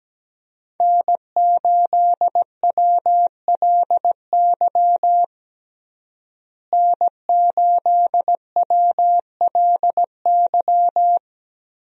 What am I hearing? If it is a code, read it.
Morse code, heard as a keyed tone, about 17 wpm, 700 Hz. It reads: N8WLY N8WLY
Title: USA Call Signs - 5 Characters - Repeated 3x - 17wpm